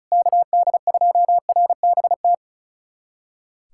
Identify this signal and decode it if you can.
Morse code, heard as a keyed tone, about 36 wpm, 700 Hz. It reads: KD2RBT
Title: USA Callsign DE Callsign - 6 Characters - 35wpm